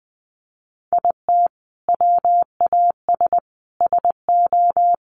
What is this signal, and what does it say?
Morse code, heard as a keyed tone, about 20 wpm, 700 Hz. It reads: IT WAS SO
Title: Sentences from Top 1400 Words 20wpm